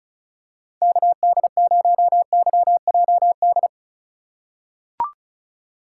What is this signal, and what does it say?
Morse code, heard as a keyed tone, about 34 wpm, 700 Hz. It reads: KD0YJD
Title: USA Call Signs - 6 Characters - 4x Character Spacing - 35wpm